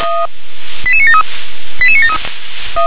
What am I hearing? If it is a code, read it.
Five-tone selective call (ZVEI), heard as quick runs of five tones, tones of about 70 ms. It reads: 88973 88973
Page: SDR/ZVEI